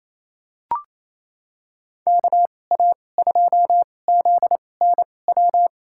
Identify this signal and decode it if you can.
Morse code, heard as a keyed tone, about 28 wpm, spaced as for 20 wpm, 700 Hz. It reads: KA2ZNW
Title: USA Call Signs - 6 Characters - 2x Character Spacing - 28wpm